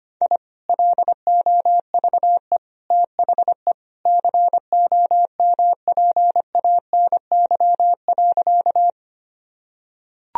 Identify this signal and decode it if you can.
Morse code, heard as a keyed tone, about 25 wpm, 700 Hz. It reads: I LOVE THE COMPANY.
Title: Sentences from Top 1000 Words - Review Part 3 - 25wpm